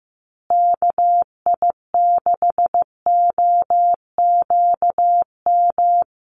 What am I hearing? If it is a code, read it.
Morse code, heard as a keyed tone, about 15 wpm, 700 Hz. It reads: KI6OQM
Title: USA Call Signs Repeated 3x 15wpm